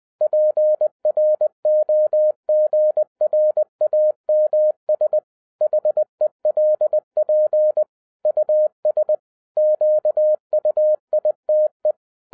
Morse code, heard as a keyed tone, about 20 wpm, 610 Hz. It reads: PROGRAMS HELP US QUITE
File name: 2012-01-mp3 / n05-CAME BACK TO HAM RADIO AFTER 30 YEARS OF INTERVAL-20wpm